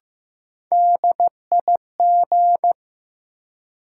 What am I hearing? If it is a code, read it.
Morse code, heard as a keyed tone, about 16 wpm, 700 Hz. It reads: DIG
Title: Sentences from Top 1600 Words - New Word Drill - 15wpm